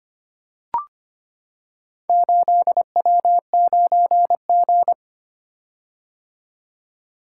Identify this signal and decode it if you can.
Morse code, heard as a keyed tone, about 25 wpm, 690 Hz. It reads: 8W9G
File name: practice / 25wpm / Four Letter-Number Combinations - 25wpm